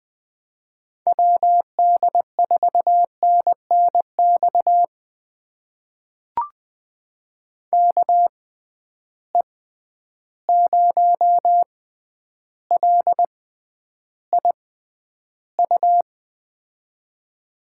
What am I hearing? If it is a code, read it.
Morse code, heard as a keyed tone, about 20 wpm, spaced as for 10 wpm, 700 Hz. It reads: WD4NNX KE0LIU
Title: USA Call Signs - 6 Characters - 6x Character Spacing - 20wpm